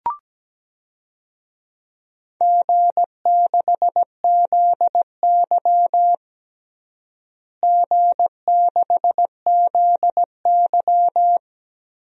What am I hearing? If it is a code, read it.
Morse code, heard as a keyed tone, about 17 wpm, 700 Hz. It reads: G6ZY G6ZY
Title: DXCC-100 Call Signs - 4 Characters - Repeated 2x - 17wpm